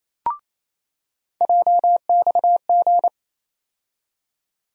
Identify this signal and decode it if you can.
Morse code, heard as a keyed tone, about 28 wpm, 700 Hz. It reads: JXG
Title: USA Call Sign Postfixes - 3 Letters 28wpm